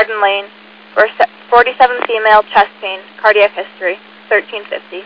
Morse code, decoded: none

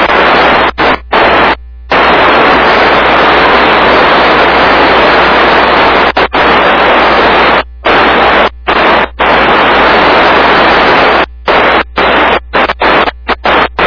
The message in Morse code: none

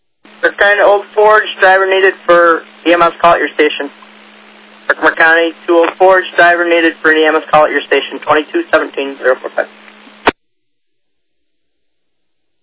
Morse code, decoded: none

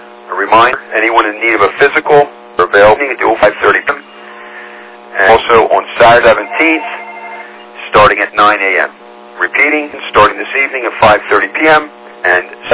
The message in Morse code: none

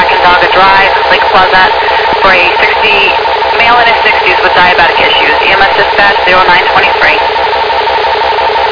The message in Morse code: none

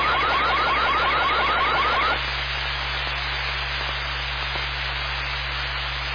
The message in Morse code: none

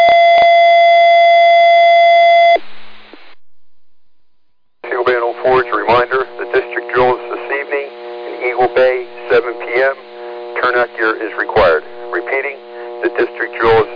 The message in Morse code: none

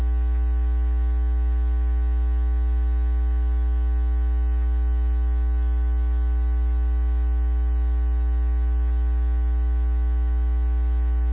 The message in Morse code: none

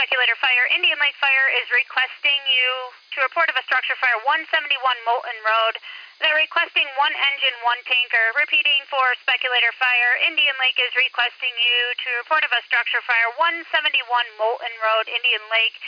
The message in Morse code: none

